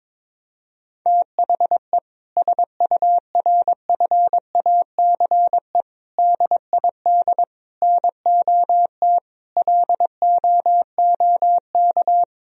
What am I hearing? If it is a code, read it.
Morse code, heard as a keyed tone, about 22 wpm, 700 Hz. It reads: THE SURFACE DID NOT LOOK
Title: Sentences from Top 1000 Words - Review Part 3 - 22wpm